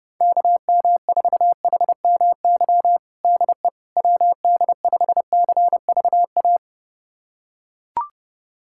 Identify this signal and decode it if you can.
Morse code, heard as a keyed tone, about 32 wpm, 700 Hz. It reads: KM4HMY DE WD5CVA
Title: USA Callsign DE Callsign - 6 Characters - 30wpm